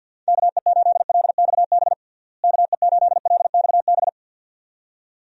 Morse code, heard as a keyed tone, about 50 wpm, 700 Hz. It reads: KE8LXB KE8LXB
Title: USA Call Signs - 6 Characters - Repeated 2x - 50wpm